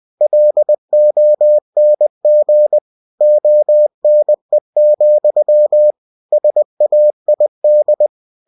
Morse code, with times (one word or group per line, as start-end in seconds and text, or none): 0.2-2.8 LONG
3.2-5.9 ONE,
6.3-8.1 SAID